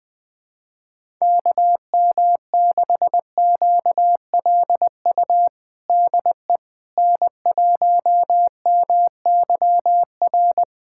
Morse code, heard as a keyed tone, about 20 wpm, 700 Hz. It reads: KM6QLU DE N1MYR
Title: USA Callsign DE Callsign 20wpm